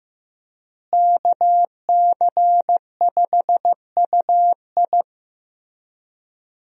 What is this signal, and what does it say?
Morse code, heard as a keyed tone, about 15 wpm, 700 Hz. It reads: KC5UI
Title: USA Call Signs - 5 Characters - Repeated 3x - 15wpm